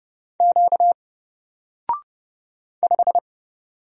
Morse code, heard as a keyed tone, about 30 wpm, 700 Hz. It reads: Q 5
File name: Single-Letter-Number-30wpm